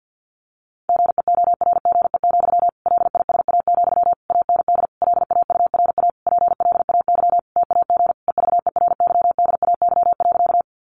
Morse code, heard as a keyed tone, about 50 wpm, 700 Hz. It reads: GEORGE, LISA, AND LAURA PLAY TAG EVERYDAY.
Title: Additional Easy Phrases 50wpm